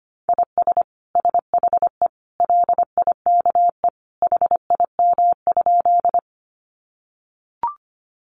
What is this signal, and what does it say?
Morse code, heard as a keyed tone, about 25 wpm, 700 Hz. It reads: IS SHE LIKE HIM?
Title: Sentences from Top 100 Words 25wpm